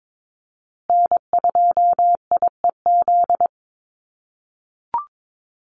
Morse code, heard as a keyed tone, about 22 wpm, 700 Hz. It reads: N2IEZ
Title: USA Call Signs Repeated 3x 22wpm